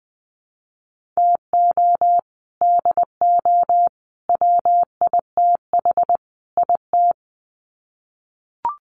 Morse code, heard as a keyed tone, about 20 wpm, 700 Hz. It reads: TO DO WITH IT
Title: Sets of 4 Words 20wpm